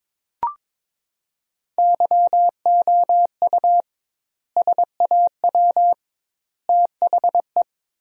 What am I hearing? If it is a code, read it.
Morse code, heard as a keyed tone, about 22 wpm, 700 Hz. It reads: YOU SAW THE
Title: 2x Word-Spacing - Sentences from Top 400 Words - 22wpm